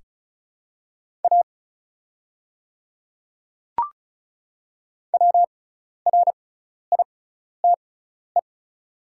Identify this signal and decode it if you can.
Morse code, heard as a keyed tone, about 36 wpm, spaced as for 12 wpm, 700 Hz. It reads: A WRITE
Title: 6x Character Spacing - Top 100 Words - 35wpm